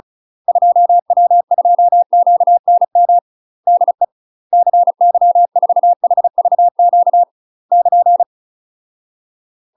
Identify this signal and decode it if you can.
Morse code, heard as a keyed tone, about 36 wpm, 700 Hz. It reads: 2W2QNM DE CY4HVQ <KN>
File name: UK MASTER QSOs TRAINING / 35WPM / UK MASTER QSOs 0031